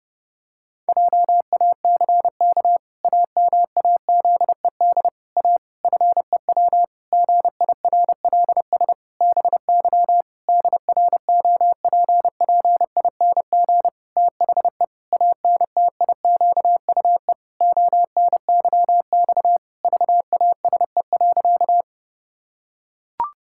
Morse code, heard as a keyed tone, about 28 wpm, 700 Hz. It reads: JACK AMAZED A FEW GIRLS BY DROPPING THE ANTIQUE ONYX VASE.
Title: Pangram Challenge v2 - 30wpm